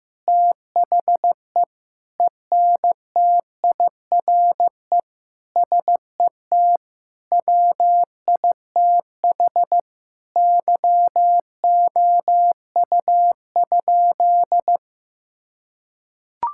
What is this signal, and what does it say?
Morse code, heard as a keyed tone, about 15 wpm, 700 Hz. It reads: THE ENTIRE SET WITH YOU?
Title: Sentences from Top 1000 Words - Review Part 5 - 15wpm